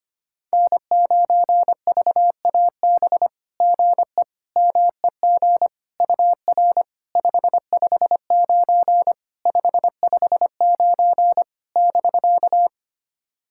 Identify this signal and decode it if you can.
Morse code, heard as a keyed tone, about 25 wpm, 690 Hz. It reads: N9VAB GE MEG UR 559 559 <BK>
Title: SOTA QSO Practice 25wpm